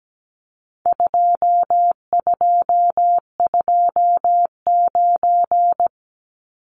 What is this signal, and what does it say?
Morse code, heard as a keyed tone, about 17 wpm, 700 Hz. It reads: 2229
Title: Numbers - 4-Digit 17wpm